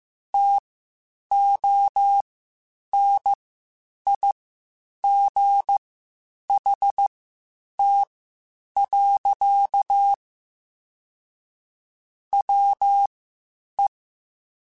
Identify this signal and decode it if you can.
Morse code, heard as a keyed tone, about 15 wpm, spaced as for 8 wpm, 800 Hz. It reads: TONIGHT. WE
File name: (spaced) / QSOs Spaced-mp3s / WD0BGP 1 @13 sp